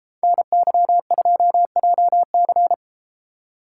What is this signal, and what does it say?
Morse code, heard as a keyed tone, about 30 wpm, 700 Hz. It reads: NY2JC